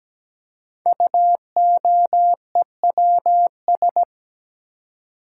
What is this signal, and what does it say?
Morse code, heard as a keyed tone, about 17 wpm, 700 Hz. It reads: UOEWS
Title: Code Groups - 5 Letters - 17wpm